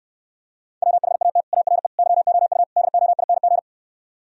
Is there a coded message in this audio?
Yes: Morse code from 0.8 to 3.6 s, heard as a keyed tone, about 68 wpm, 700 Hz. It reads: WHAT ARE YOU DOING